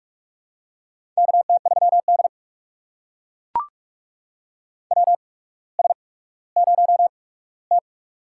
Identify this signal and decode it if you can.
Morse code, heard as a keyed tone, about 42 wpm, spaced as for 19 wpm, 700 Hz. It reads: KT3D WS0T
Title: USA Call Signs - 4 Characters - 8x Character Spacing - 45wpm